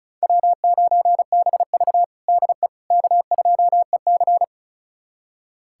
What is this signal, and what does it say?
Morse code, heard as a keyed tone, about 36 wpm, 700 Hz. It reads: W9BV DE K2EC
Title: USA Callsign DE Callsign - 4 Characters - 35wpm